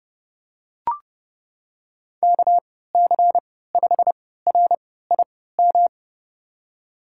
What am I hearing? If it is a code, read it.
Morse code, heard as a keyed tone, about 30 wpm, spaced as for 17 wpm, 700 Hz. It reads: KC5RIM
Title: USA Call Signs - 6 Characters - 3x Character Spacing - 30wpm